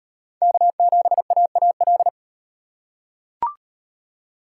Morse code, heard as a keyed tone, about 32 wpm, 700 Hz. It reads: K7AAL